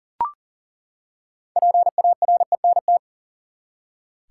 Morse code, heard as a keyed tone, about 40 wpm, 700 Hz. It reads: PARENT
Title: Six Letter Words - 40wpm